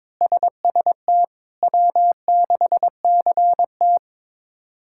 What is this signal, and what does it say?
Morse code, heard as a keyed tone, about 22 wpm, 700 Hz. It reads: SST W6CT
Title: K1USN SST QSO Practice - 22wpm